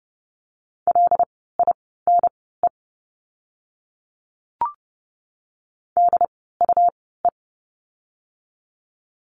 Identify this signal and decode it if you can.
Morse code, heard as a keyed tone, about 29 wpm, spaced as for 17 wpm, 700 Hz. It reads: LINE DUE
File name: Lesson 17f - Letter u - Words 30wpm